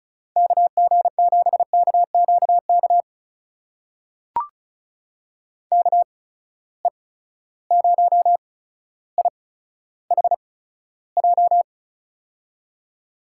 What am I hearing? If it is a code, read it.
Morse code, heard as a keyed tone, about 36 wpm, spaced as for 15 wpm, 700 Hz. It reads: KG7KQK KE0IHJ
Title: USA Call Signs - 6 Characters - 8x Character Spacing - 35wpm